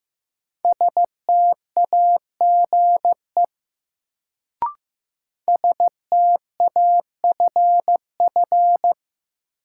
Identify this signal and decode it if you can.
Morse code, heard as a keyed tone, about 15 wpm, 700 Hz. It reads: STAGE STAFF
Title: Sentences from Top 1200 Words - New Word Drill Alt - 15wpm